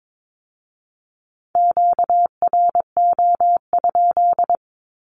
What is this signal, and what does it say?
Morse code, heard as a keyed tone, about 22 wpm, 700 Hz. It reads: QRO?
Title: Q-Codes with Spoken Meaning - 22wpm